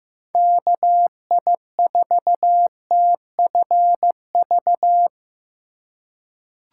Morse code, heard as a keyed tone, about 16 wpm, 700 Hz. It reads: KI4TFV
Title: USA Call Signs - 4x Character Spacing - 15wpm